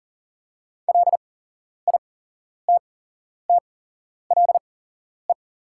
Morse code, heard as a keyed tone, about 40 wpm, spaced as for 11 wpm, 690 Hz. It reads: LITTLE
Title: 8x Character Spacing - Top 500 Words - 40wpm